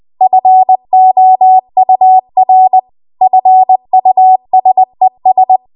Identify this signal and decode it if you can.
Morse code, heard as a keyed tone, about 20 wpm, 750 Hz. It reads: FOUR FUSES
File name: Morse / 20WPM